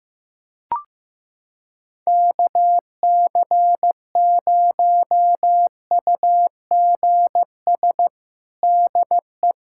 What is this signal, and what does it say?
Morse code, heard as a keyed tone, about 15 wpm, 700 Hz. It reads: KC0UGS DE
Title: USA Callsign DE Callsign - 6 Characters - 15wpm